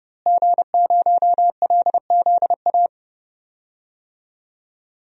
Morse code, heard as a keyed tone, about 30 wpm, 700 Hz. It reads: G0LZA
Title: DXCC-100 Call Signs - 5 Characters - Repeated 2x - 30wpm